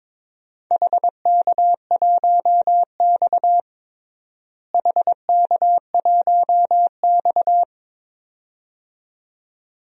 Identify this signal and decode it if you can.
Morse code, heard as a keyed tone, about 21 wpm, 700 Hz. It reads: HK1X HK1X
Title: DXCC-100 Call Signs - 4 Characters - Repeated 2x - 22wpm